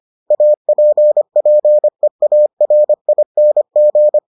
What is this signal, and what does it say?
Morse code, heard as a keyed tone, about 25 wpm, 600 Hz. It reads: APPEARING